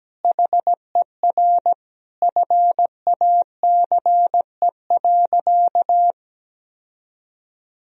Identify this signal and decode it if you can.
Morse code, heard as a keyed tone, about 17 wpm, 700 Hz. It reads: HER FACE.
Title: Sentences from Top 1000 Words - Review Part 2 - 17wpm